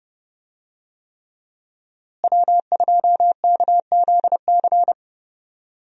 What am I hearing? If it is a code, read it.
Morse code, heard as a keyed tone, about 30 wpm, 700 Hz. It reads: W2KZC
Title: USA Call Signs - 5 Characters - Repeated 3x - 30wpm